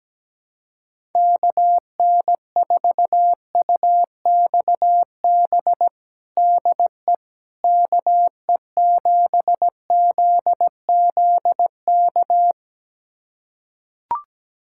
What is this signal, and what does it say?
Morse code, heard as a keyed tone, about 17 wpm, 700 Hz. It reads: KN4UXB DE KE7ZZK
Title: USA Callsign DE Callsign - 6 Characters - 17wpm